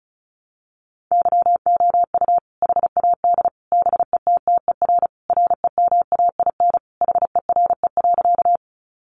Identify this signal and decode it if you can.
Morse code, heard as a keyed tone, about 34 wpm, 700 Hz. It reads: YOU HAD BETTER REMAIN HERE.
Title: Sentences from Top 1200 Words 35wpm